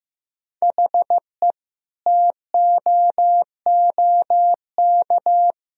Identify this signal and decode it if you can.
Morse code, heard as a keyed tone, about 15 wpm, 700 Hz. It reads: HE TOOK
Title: Sentences from Top 1000 Words - Review Part 2 - 15wpm